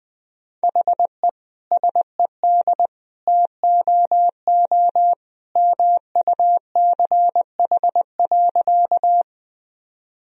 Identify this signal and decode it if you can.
Morse code, heard as a keyed tone, about 20 wpm, 700 Hz. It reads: HE SED TOO MUCH.
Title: Sentences from Top 200 Words and Abbreviations 20wpm